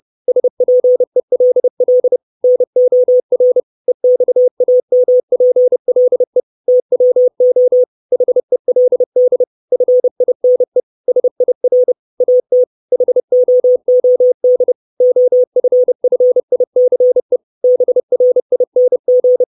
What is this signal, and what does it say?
Morse code, heard as a keyed tone, about 30 wpm, 500 Hz. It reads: SPELL NOR EXAMPLE TWO HELD FINE SIR AT HOOD OFFICE BRING
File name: 632 Words 30WPM